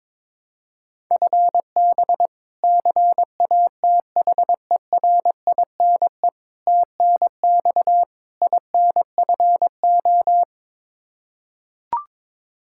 Morse code, heard as a keyed tone, about 22 wpm, 700 Hz. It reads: FB CATHERINE TNX INFO
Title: K1USN SST QSO Practice 22wpm